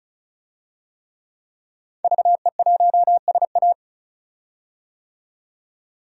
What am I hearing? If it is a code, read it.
Morse code, heard as a keyed tone, about 34 wpm, 700 Hz. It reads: VE1SA
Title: DXCC-100 Call Signs - 5 Characters - 35wpm